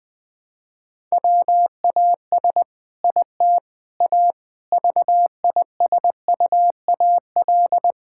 Morse code, heard as a keyed tone, about 20 wpm, 700 Hz. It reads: WAS IT A VISUAL
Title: Sentences from Top 1000 Words - Review Part 6 - 20wpm